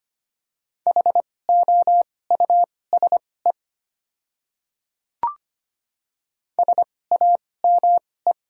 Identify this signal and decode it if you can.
Morse code, heard as a keyed tone, about 25 wpm, spaced as for 18 wpm, 690 Hz. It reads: HOUSE SAME
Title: 2x Character Spacing - Top 200 Words - 25wpm